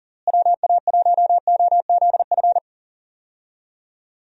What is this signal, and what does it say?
Morse code, heard as a keyed tone, about 38 wpm, 700 Hz. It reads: WA1OZF
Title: USA Call Signs - 6 Characters - Repeated 3x - 40wpm